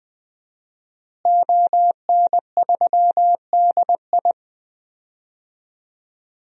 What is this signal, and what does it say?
Morse code, heard as a keyed tone, about 20 wpm, 700 Hz. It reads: ON3DI
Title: DXCC-100 Call Signs - 5 Characters - Repeated 3x - 20wpm